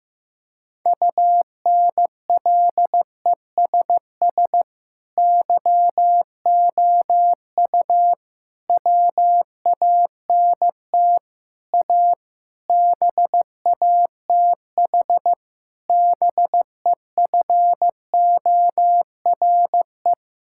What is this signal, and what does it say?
Morse code, heard as a keyed tone, about 15 wpm, 700 Hz. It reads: UNLESS YOU WANT A BATH BEFORE
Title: Sentences from Top 1600 Words 15wpm